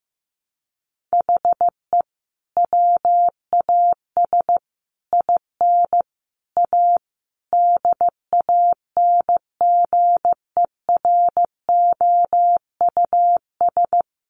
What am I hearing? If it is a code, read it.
Morse code, heard as a keyed tone, about 15 wpm, 700 Hz. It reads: HE WAS IN A DANGEROUS
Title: Sentences from Top 1200 Words 15wpm